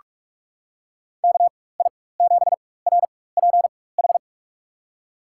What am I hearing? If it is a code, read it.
Morse code, heard as a keyed tone, about 44 wpm, spaced as for 21 wpm, 700 Hz. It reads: KI7RPH
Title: USA Call Signs - 6 Characters - 4x Character Spacing - 45wpm